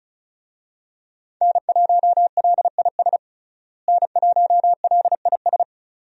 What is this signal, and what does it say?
Morse code, heard as a keyed tone, about 36 wpm, 700 Hz. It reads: N1LIS N1LIS
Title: USA Call Signs - 5 Characters - Repeated 3x - 35wpm